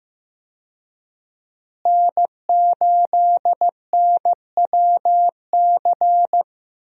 Morse code, heard as a keyed tone, about 16 wpm, 700 Hz. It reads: N8NWC
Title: USA Call Signs - 5 Characters - Repeated 3x - 15wpm